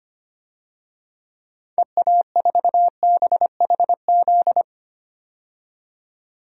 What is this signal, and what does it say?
Morse code, heard as a keyed tone, about 25 wpm, 690 Hz. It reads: EA4BHZ
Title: DXCC-100 Call Signs - 6 Characters - 25wpm